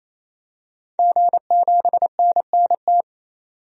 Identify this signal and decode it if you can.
Morse code, heard as a keyed tone, about 28 wpm, 700 Hz. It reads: G7NNT
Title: DXCC-100 Call Signs - 5 Characters - 28wpm